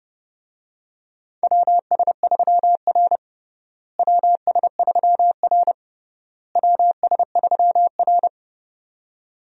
Morse code, heard as a keyed tone, about 32 wpm, 700 Hz. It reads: WS3R WS3R WS3R
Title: USA Call Signs - 4 Characters - Repeated 3x - 30wpm